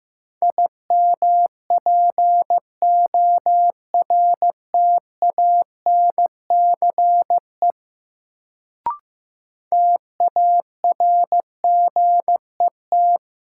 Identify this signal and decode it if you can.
Morse code, heard as a keyed tone, about 15 wpm, 700 Hz. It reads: IMPORTANCE TARGET
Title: Sentences from Top 1200 Words - New Word Drill - 15wpm